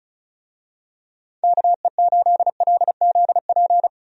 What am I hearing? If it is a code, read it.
Morse code, heard as a keyed tone, about 36 wpm, 700 Hz. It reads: KE8LZP
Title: USA Call Signs - 6 Characters - 35wpm